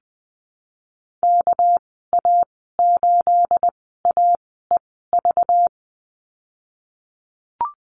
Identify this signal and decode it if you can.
Morse code, heard as a keyed tone, about 20 wpm, spaced as for 14 wpm, 700 Hz. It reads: KA8AEV
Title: 245 Lesson 39g - Letter z - Call Signs 20wpm